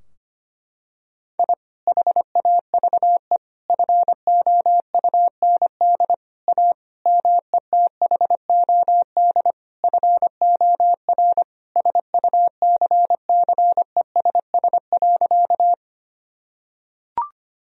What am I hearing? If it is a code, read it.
Morse code, heard as a keyed tone, about 25 wpm, 700 Hz. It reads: I HAVE FOUND A METHOD FOR SUCCESS.
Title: Sentences from Top 1000 Words - Review Part 3 - 25wpm